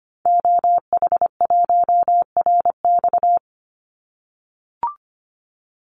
Morse code, heard as a keyed tone, about 25 wpm, 700 Hz. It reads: OH1RX
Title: Call Signs - 25wpm